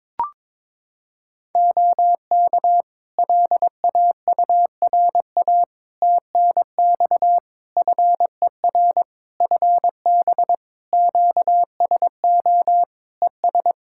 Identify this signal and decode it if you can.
Morse code, heard as a keyed tone, about 22 wpm, 700 Hz. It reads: OK LAURA TNX FER FB QSO ES
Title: K1USN SST QSO Practice 22wpm